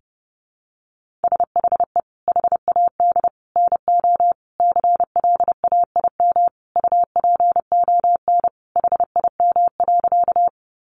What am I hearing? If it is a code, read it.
Morse code, heard as a keyed tone, about 30 wpm, 700 Hz. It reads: SHE HAD NO CLAIM UPON HIM.
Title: Sentences from Top 1500 Words 30wpm